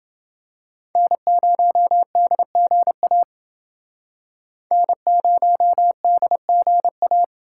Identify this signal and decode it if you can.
Morse code, heard as a keyed tone, about 28 wpm, 700 Hz. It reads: N0DGA N0DGA